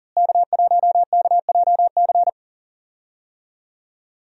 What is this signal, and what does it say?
Morse code, heard as a keyed tone, about 40 wpm, 700 Hz. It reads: K1KJC
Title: USA Call Signs - 5 Characters - Repeated 3x - 40wpm